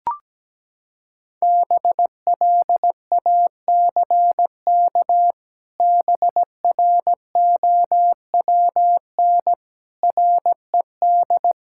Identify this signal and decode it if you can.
Morse code, heard as a keyed tone, about 17 wpm, 700 Hz. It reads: BLACK BROWN RED
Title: Pangram Challenge v2 - 17wpm